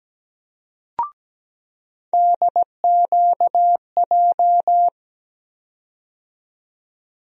Morse code, heard as a keyed tone, about 17 wpm, 700 Hz. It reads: DQJ
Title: USA Call Sign Postfixes - 3 Letters 17wpm